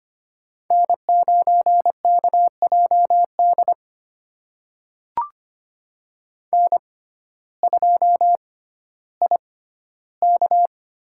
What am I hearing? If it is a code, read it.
Morse code, heard as a keyed tone, about 25 wpm, spaced as for 14 wpm, 700 Hz. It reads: N9KJD N2IK
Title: USA Call Signs - 6x Character Spacing - 25wpm